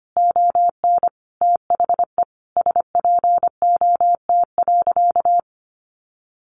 Morse code, heard as a keyed tone, about 25 wpm, 700 Hz. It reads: ON THE SPOT.
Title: Sentences from Top 1200 Words 25wpm